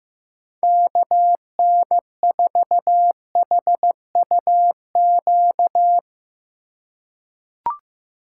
Morse code, heard as a keyed tone, about 15 wpm, 700 Hz. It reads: KN4HUQ
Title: USA Call Signs - 2x Character Spacing - 15wpm